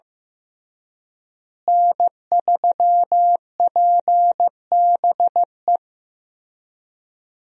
Morse code, heard as a keyed tone, about 15 wpm, 700 Hz. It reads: N3PBE
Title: USA Call Signs - 5 Characters - Repeated 3x - 15wpm